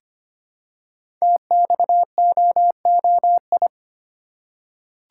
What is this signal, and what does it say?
Morse code, heard as a keyed tone, about 25 wpm, 700 Hz. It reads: TXOOI
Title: Code Groups - 5 Letters - 25wpm